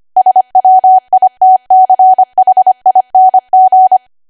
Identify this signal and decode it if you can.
Morse code, heard as a keyed tone, about 25 wpm, 750 Hz. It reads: SWITCHING